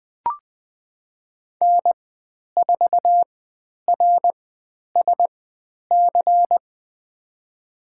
Morse code, heard as a keyed tone, about 20 wpm, spaced as for 10 wpm, 700 Hz. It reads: N4RSC
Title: 095 Lesson 16g - Letter c - Call Signs 20wpm